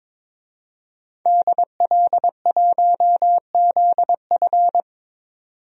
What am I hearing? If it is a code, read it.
Morse code, heard as a keyed tone, about 21 wpm, 700 Hz. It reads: DL1ZF
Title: DXCC-100 Call Signs - 5 Characters - 22wpm